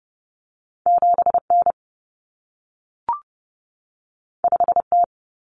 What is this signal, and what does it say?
Morse code, heard as a keyed tone, about 30 wpm, 700 Hz. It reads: 7N 5T
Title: Common Cut Numbers - 2-Digit 30wpm